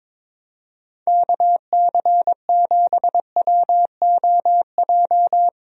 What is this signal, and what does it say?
Morse code, heard as a keyed tone, about 22 wpm, 700 Hz. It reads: KC7WOJ
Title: USA Call Signs - 6 Characters - Repeated 3x - 22wpm